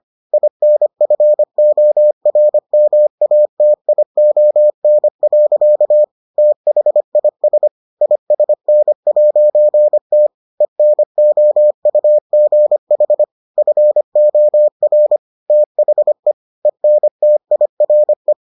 Morse code, heard as a keyed tone, about 25 wpm, 600 Hz. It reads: INFORMATION. THIS ISN'T ENOUGH FOR THE ENTIRE